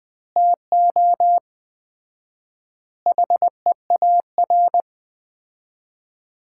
Morse code, heard as a keyed tone, about 20 wpm, 700 Hz. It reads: TO HEAR
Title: 4x Word-Spacing - Sets of 5 Words - 20wpm